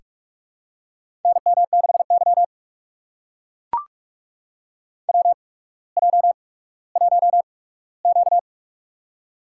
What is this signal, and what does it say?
Morse code, heard as a keyed tone, about 41 wpm, spaced as for 19 wpm, 700 Hz. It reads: NM6Y WJ1Q
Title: USA Call Signs - 4 Characters - 8x Character Spacing - 45wpm